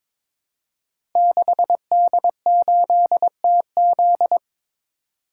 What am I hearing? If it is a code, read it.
Morse code, heard as a keyed tone, about 22 wpm, 700 Hz. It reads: 6D8TZ
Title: Code Groups - 5 Letter:Numbers - 22wpm